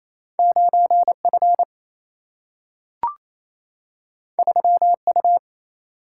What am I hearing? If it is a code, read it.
Morse code, heard as a keyed tone, about 28 wpm, 700 Hz. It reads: 9 F 3 U
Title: Single-Letter Single-Number 28wpm